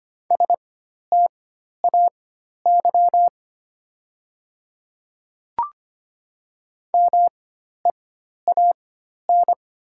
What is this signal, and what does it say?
Morse code, heard as a keyed tone, about 25 wpm, spaced as for 12 wpm, 700 Hz. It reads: STAY MEAN
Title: 4x Character Spacing - Top 500 Words - 25wpm